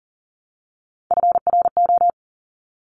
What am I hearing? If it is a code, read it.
Morse code, heard as a keyed tone, about 42 wpm, 700 Hz. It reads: FRO
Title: Top Three Letter Combinations in Words 40wpm